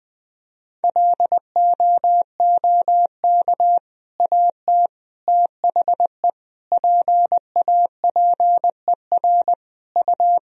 Morse code, heard as a keyed tone, about 20 wpm, 700 Hz. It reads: LOOK AT THE PAPER U
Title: Sentences from Top 400 Words and Abbreviations 20wpm